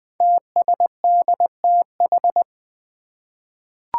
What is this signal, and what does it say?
Morse code, heard as a keyed tone, about 20 wpm, 700 Hz. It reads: TSDTH
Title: Five Challenging Letter-Number Combinations 20wpm